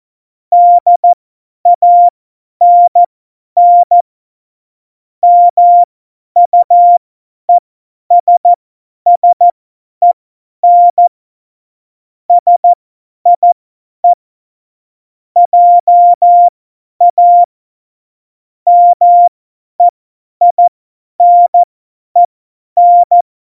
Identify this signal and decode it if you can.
Morse code, heard as a keyed tone, about 14 wpm, spaced as for 10 wpm, 700 Hz. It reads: DANN MUESSEN SIE JA MEINEN
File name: Morsekurs / Witze / Witz 15. August 17, Tempo 50 F70